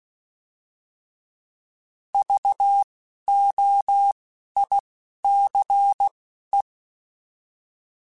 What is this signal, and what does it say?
Morse code, heard as a keyed tone, about 16 wpm, spaced as for 11 wpm, 800 Hz. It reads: VOICE
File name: Alphabetical / Alphabetical mp3s / Vs @10